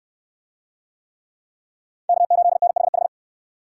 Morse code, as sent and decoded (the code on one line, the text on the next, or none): -.- ---.. -. .... ....
K8NHH